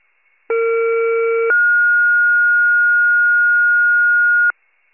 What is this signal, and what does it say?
Two-tone page, heard as a one-second tone, then a three-second tone, about 460 then 1430 Hz. It is heard from 0.5 to 4.5 s.